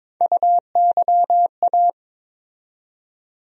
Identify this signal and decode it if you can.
Morse code, heard as a keyed tone, about 22 wpm, 700 Hz. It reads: UYA
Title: USA Call Sign Postfixes - 3 Letters 22wpm